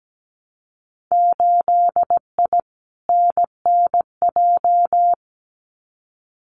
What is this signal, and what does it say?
Morse code, heard as a keyed tone, about 17 wpm, 700 Hz. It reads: 8I NNJ
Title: Winter Field Day Exchanges Encore 17wpm